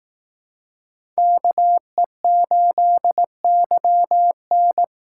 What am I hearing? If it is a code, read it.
Morse code, heard as a keyed tone, about 18 wpm, 700 Hz. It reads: KE8YN
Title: Speed Racing - USA 2x2 Call Signs 15wpm